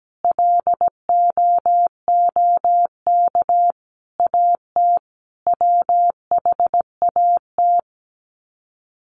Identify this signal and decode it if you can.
Morse code, heard as a keyed tone, about 17 wpm, 700 Hz. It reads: LOOK AT WHAT
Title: Sets of 5 Words Encore2 17wpm